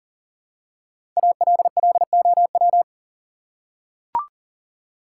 Morse code, heard as a keyed tone, about 37 wpm, 700 Hz. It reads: ALLOW
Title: Five Letter Words - 40wpm